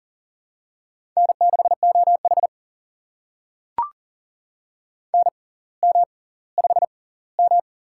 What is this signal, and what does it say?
Morse code, heard as a keyed tone, about 40 wpm, spaced as for 20 wpm, 700 Hz. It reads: N6OH NM5M
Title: USA Call Signs - 4 Characters - 6x Character Spacing - 40wpm